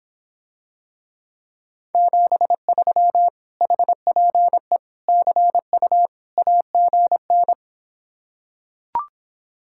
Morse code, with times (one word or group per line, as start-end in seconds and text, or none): 1.9-3.3 73
3.6-4.8 HPE
5.1-6.1 CU
6.4-7.5 AGN